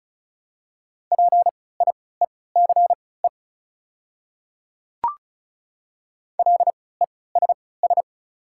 Morse code, heard as a keyed tone, about 34 wpm, spaced as for 20 wpm, 700 Hz. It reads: PIECE LESS
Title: 3x Character Spacing - Top 500 Words - 35wpm